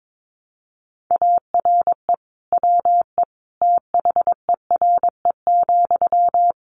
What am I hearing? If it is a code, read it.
Morse code, heard as a keyed tone, about 22 wpm, 700 Hz. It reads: ARE WE THERE,
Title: Sentences from Top 1300 Words 22wpm